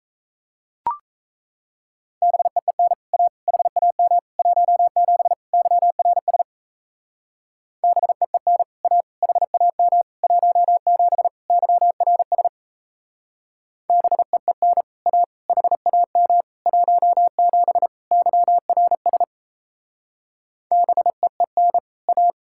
Morse code, heard as a keyed tone, about 35 wpm, 700 Hz. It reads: BEEN A HAM 17 YRS BEEN A HAM 17 YRS BEEN A HAM 17 YRS BEEN A